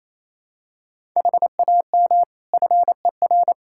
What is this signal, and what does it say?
Morse code, heard as a keyed tone, about 28 wpm, 700 Hz. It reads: HAM FER
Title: Practice QSO - Exchange 3 - More Info - 28wpm